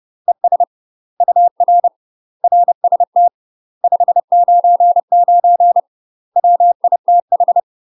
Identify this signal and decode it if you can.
Morse code, heard as a keyed tone, about 30 wpm, 700 Hz. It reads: ES UR RST 599 WITH
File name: UK MASTER QSOs TRAINING / 30WPM / UK MASTER QSOs 0000